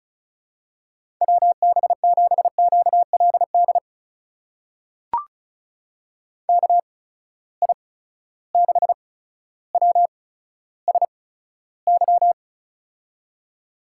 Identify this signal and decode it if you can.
Morse code, heard as a keyed tone, about 34 wpm, spaced as for 15 wpm, 700 Hz. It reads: WB7QLD KI6WSY
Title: USA Call Signs - 6 Characters - 8x Character Spacing - 35wpm